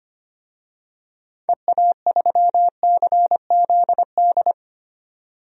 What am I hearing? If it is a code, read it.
Morse code, heard as a keyed tone, about 25 wpm, 700 Hz. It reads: EA3CZD
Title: DXCC-100 Call Signs - 6 Characters - 25wpm